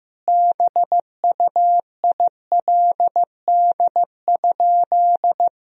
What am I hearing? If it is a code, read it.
Morse code, heard as a keyed tone, about 15 wpm, 700 Hz. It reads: BUILD?
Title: Sentences from Top 1000 Words - Review Part 3 - 15wpm